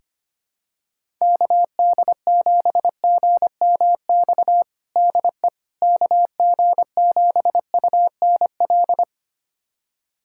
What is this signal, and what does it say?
Morse code, heard as a keyed tone, about 25 wpm, 700 Hz. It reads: KD7GMX DE KG7UNL
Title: USA Callsign DE Callsign - 6 Characters - 25wpm